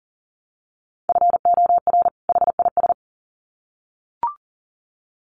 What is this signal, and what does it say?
Morse code, heard as a keyed tone, about 40 wpm, 700 Hz. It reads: FOR HIS
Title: Sets of 2 Words 40wpm